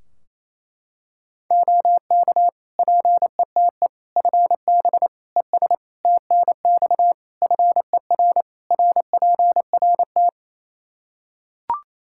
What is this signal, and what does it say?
Morse code, heard as a keyed tone, about 28 wpm, 700 Hz. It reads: OK PETE FB ES TNX FER RPRT
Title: Practice QSO - Exchange 2 - Info - 28wpm